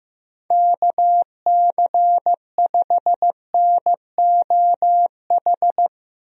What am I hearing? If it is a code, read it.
Morse code, heard as a keyed tone, about 16 wpm, 700 Hz. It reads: KC5NOH
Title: USA Call Signs - 6 Characters - Repeated 3x - 15wpm